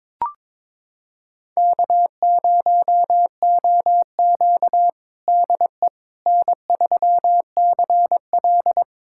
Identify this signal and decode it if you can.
Morse code, heard as a keyed tone, about 22 wpm, 700 Hz. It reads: K0OQ DE N3CL
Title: USA Callsign DE Callsign - 4 Characters - 22wpm